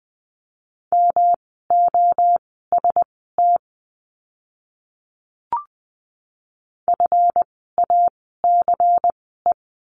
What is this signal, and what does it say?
Morse code, heard as a keyed tone, about 20 wpm, spaced as for 15 wpm, 700 Hz. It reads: MOST FACE
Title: 139 Lesson 23f - Letter f - Words 20wpm